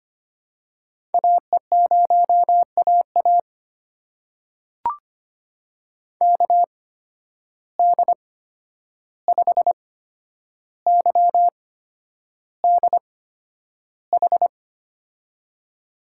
Morse code, heard as a keyed tone, about 25 wpm, spaced as for 10 wpm, 700 Hz. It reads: AE0AA KD5YDH
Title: USA Call Signs - 8x Character Spacing - 25wpm